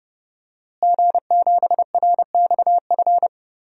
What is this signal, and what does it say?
Morse code, heard as a keyed tone, about 30 wpm, 700 Hz. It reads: G7RXF
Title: DXCC-100 Call Signs - 5 Characters - Repeated 2x - 30wpm